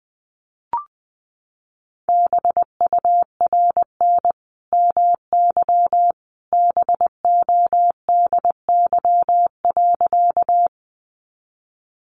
Morse code, heard as a keyed tone, about 20 wpm, 700 Hz. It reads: BURN MY BODY.
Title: Sentences from Top 1300 Words 20wpm